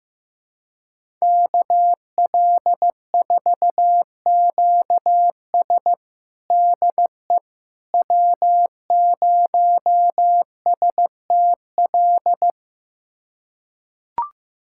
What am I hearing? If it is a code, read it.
Morse code, heard as a keyed tone, about 15 wpm, 700 Hz. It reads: KL4QS DE W0STL
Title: USA Callsign DE Callsign - 5 Characters - 15wpm